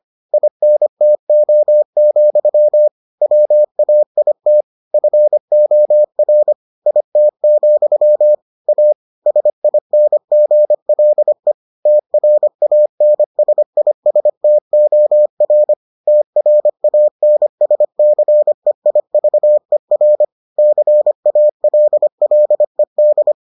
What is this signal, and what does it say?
Morse code, heard as a keyed tone, about 25 wpm, 600 Hz. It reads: INTO, WAIT FOR IT, A SINGLE TRANSISTOR TRANSCEIVER CALLED